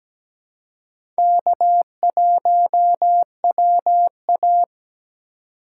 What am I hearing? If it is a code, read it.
Morse code, heard as a keyed tone, about 17 wpm, 700 Hz. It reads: K1WA
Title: USA Call Signs - 2x Character Spacing - 17wpm